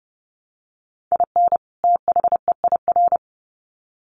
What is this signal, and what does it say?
Morse code, heard as a keyed tone, about 30 wpm, 700 Hz. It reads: IN THEIR
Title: Sets of 2 Words 30wpm